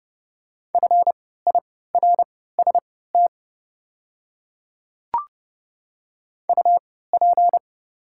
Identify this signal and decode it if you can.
Morse code, heard as a keyed tone, about 28 wpm, spaced as for 17 wpm, 690 Hz. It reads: FIRST UP
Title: 3x Character Spacing - Top 100 Words - 30wpm